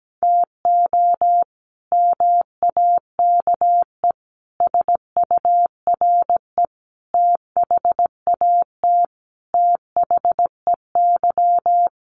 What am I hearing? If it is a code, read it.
Morse code, heard as a keyed tone, about 17 wpm, 700 Hz. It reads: TO MAKE SURE THAT THEY
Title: Sets of 5 Words Encore2 17wpm